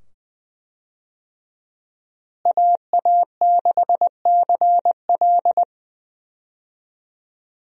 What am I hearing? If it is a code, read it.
Morse code, heard as a keyed tone, about 20 wpm, 700 Hz. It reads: AA6CL
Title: USA Call Signs - 5 Characters - Repeated 2x - 20wpm